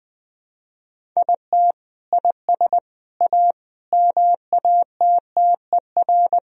Morse code, heard as a keyed tone, about 20 wpm, 700 Hz. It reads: IT IS A MATTER
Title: Sentences from Top 1000 Words - Review Part 3 - 20wpm